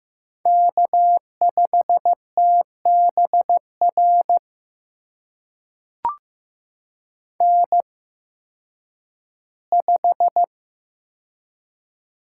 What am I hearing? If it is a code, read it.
Morse code, heard as a keyed tone, about 15 wpm, spaced as for 10 wpm, 700 Hz. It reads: K5TBR N5
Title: USA Call Signs - 8x Character Spacing - 15wpm